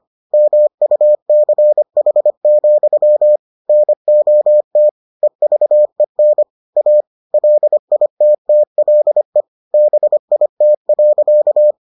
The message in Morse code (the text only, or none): MUCH, NOT EVEN A LITTLE BIT.